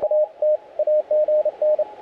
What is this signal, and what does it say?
Morse code, heard as a keyed tone, about 30 wpm, 600 Hz, with radio noise. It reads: A T AGN
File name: audio mp3 files / A71BX 40m CW 2008-4-12 2-0-53